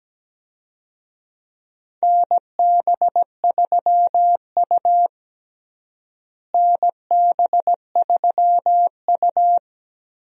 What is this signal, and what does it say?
Morse code, heard as a keyed tone, about 17 wpm, 700 Hz. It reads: NB3U NB3U
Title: USA Call Signs - 4 Characters - Repeated 2x - 17wpm